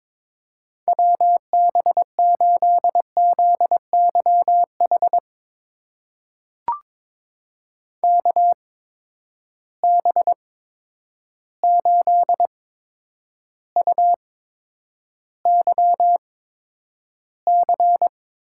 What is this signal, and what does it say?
Morse code, heard as a keyed tone, about 22 wpm, spaced as for 9 wpm, 700 Hz. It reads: WB8ZYH KB8UYC
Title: USA Call Signs - 6 Characters - 8x Character Spacing - 22wpm